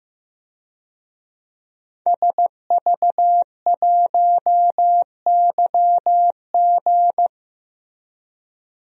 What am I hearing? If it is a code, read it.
Morse code, heard as a keyed tone, about 15 wpm, 700 Hz. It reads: SV1YG
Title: DXCC-100 Call Signs - 5 Characters - Repeated 2x - 15wpm